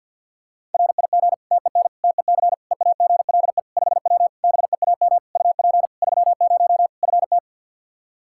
Morse code, heard as a keyed tone, about 52 wpm, 700 Hz. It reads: RIG TEN TEC EAGLE 5W BEAM UP 30 FT
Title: Practice QSO - Exchange 2 - Info - 50wpm